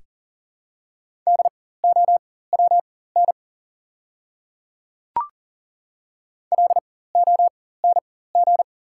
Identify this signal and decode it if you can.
Morse code, heard as a keyed tone, about 40 wpm, spaced as for 19 wpm, 700 Hz. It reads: DOWN LONG
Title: 4x Character Spacing - Top 100 Words - 40wpm